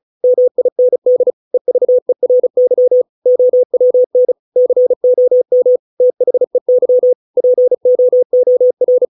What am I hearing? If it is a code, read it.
Morse code, heard as a keyed tone, about 34 wpm, 500 Hz. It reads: MIND EVERY OWN COM THEY POOR